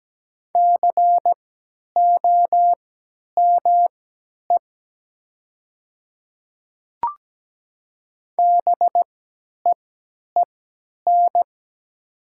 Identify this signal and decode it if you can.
Morse code, heard as a keyed tone, about 17 wpm, spaced as for 10 wpm, 700 Hz. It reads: COME BEEN
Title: 3x Character Spacing - Top 100 Words - 17wpm